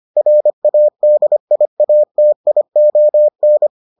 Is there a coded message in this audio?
Yes: Morse code from 0.2 to 3.7 s, heard as a keyed tone, about 25 wpm, 600 Hz. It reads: RADIATION